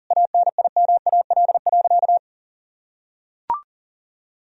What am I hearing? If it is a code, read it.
Morse code, heard as a keyed tone, about 40 wpm, 700 Hz. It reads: ANIMAL.
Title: Sentences from Top 1000 Words - Review Part 1 - 40wpm